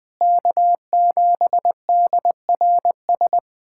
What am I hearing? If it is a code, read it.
Morse code, heard as a keyed tone, about 20 wpm, 700 Hz. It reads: K7DRS
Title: USA Callsign DE Callsign - 5 Characters - 20wpm